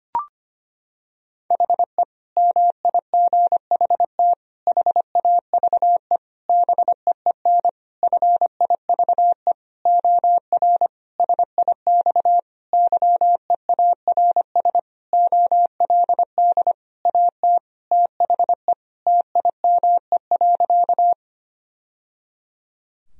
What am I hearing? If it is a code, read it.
Morse code, heard as a keyed tone, about 25 wpm, 700 Hz. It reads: HE MIGHT HAVE BEEN FIVE OR SIX YEARS OLD AT THE TIME.
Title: Sentences from Top 1000 Words - Review Part 2 - 25wpm